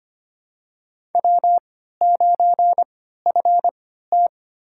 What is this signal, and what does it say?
Morse code, heard as a keyed tone, about 25 wpm, spaced as for 14 wpm, 700 Hz. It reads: W9FT
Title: USA Call Signs - 4 Characters - 3x Character Spacing - 25wpm